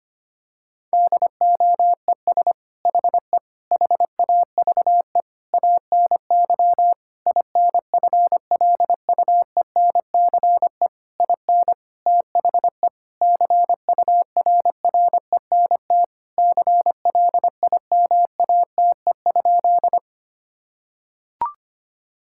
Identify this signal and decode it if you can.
Morse code, heard as a keyed tone, about 25 wpm, 700 Hz. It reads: DOES HE HAVE ANY INFLUENCE IN THE CURRENT CLIMATE?
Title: Sentences from Top 1600 Words 25wpm